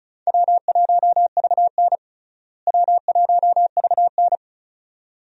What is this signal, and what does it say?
Morse code, heard as a keyed tone, about 36 wpm, 700 Hz. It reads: W1VN W1VN
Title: USA Call Signs - 4 Characters - Repeated 2x - 35wpm